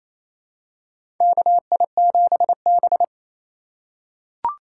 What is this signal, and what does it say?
Morse code, heard as a keyed tone, about 28 wpm, 700 Hz. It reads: KI7B
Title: USA Call Signs - 4 Characters - 4x Character Spacing - 28wpm